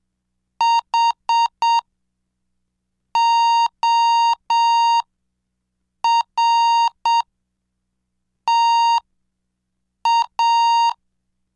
Morse code, heard as a keyed tone, about 7 wpm, spaced as for 5 wpm, 930 Hz. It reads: HORTA
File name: cursos / CDestudo / 17 Track 17